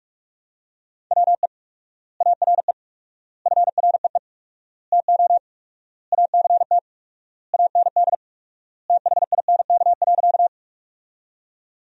Morse code, heard as a keyed tone, about 46 wpm, 700 Hz. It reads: WE ARE FREE TO ACT AND THINK.
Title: 4x Word-Spacing - Sentences from Top 500 Words - 45wpm